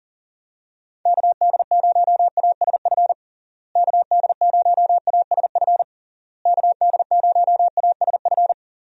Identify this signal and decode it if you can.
Morse code, heard as a keyed tone, about 40 wpm, 700 Hz. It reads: KD0ASF KD0ASF KD0ASF
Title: USA Call Signs - 6 Characters - Repeated 3x - 40wpm